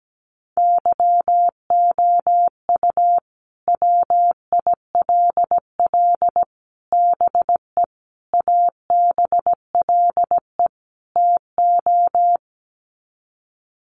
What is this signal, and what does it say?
Morse code, heard as a keyed tone, about 17 wpm, 700 Hz. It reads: YOU WILL BE ABLE TO
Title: Sets of 5 Words Encore2 17wpm